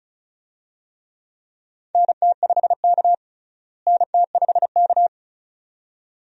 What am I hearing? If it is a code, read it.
Morse code, heard as a keyed tone, about 36 wpm, 700 Hz. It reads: NT5K NT5K
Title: USA Call Signs - 4 Characters - Repeated 2x - 35wpm